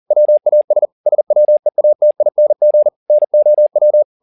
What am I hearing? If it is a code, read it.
Morse code, heard as a keyed tone, about 40 wpm, 600 Hz. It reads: WAS SWEATING NOW
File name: the old man and the sea 2 40wpm